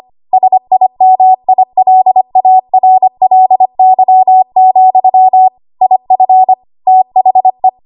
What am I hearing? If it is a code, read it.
Morse code, heard as a keyed tone, about 25 wpm, 750 Hz. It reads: SIMILARLY, IF THE
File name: Archive / 25 WPM / 25WPM